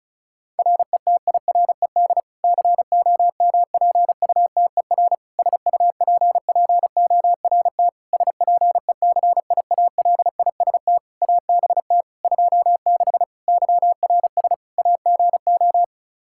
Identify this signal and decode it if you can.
Morse code, heard as a keyed tone, about 34 wpm, 700 Hz. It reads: RETIRED COMPUTER SUPPORT SPECIALIST ABT 26 YRS AGO
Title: Practice QSO - Exchange 3 - More Info - 35wpm